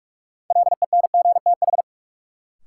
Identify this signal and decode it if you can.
Morse code, heard as a keyed tone, about 46 wpm, 700 Hz. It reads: LENGTH
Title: Sentences from Top 1000 Words - New Word Drill - 45wpm